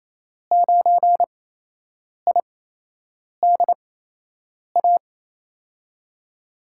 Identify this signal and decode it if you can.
Morse code, heard as a keyed tone, about 28 wpm, spaced as for 8 wpm, 700 Hz. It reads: 9IDA
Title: USA Call Signs - 5 Characters - 8x Character Spacing - 28wpm